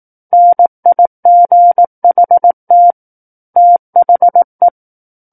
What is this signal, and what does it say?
Morse code, heard as a keyed tone, about 18 wpm, 700 Hz. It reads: NIGHT THE
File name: 18 WPM